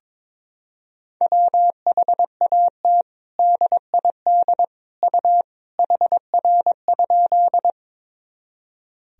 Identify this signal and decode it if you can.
Morse code, heard as a keyed tone, about 22 wpm, 700 Hz. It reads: WHAT DID U HR?
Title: Sentences from Top 300 Words and Abbreviations 22wpm